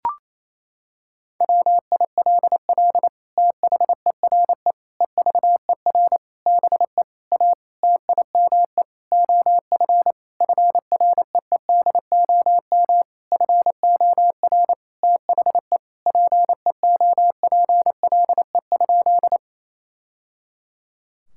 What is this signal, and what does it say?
Morse code, heard as a keyed tone, about 28 wpm, 700 Hz. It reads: WILL THERE EVER BE A TIME OF FREEDOM FOR THE PEOPLE?
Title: Sentences from Top 1000 Words - Review Part 5 - 28wpm